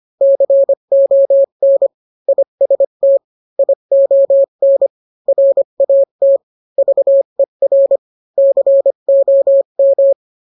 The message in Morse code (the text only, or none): CON IST ION RAT VER COM